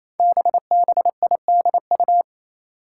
Morse code, heard as a keyed tone, about 29 wpm, 700 Hz. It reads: BBIDU
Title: Five Challenging Letter-Number Combinations 28wpm